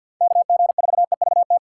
Morse code, heard as a keyed tone, about 53 wpm, 700 Hz. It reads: KG4EVT
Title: USA Call Signs - 6 Characters - 6x Character Spacing - 50wpm